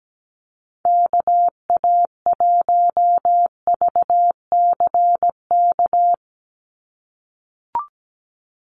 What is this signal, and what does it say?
Morse code, heard as a keyed tone, about 17 wpm, 700 Hz. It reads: KA1VCK
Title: USA Call Signs 17wpm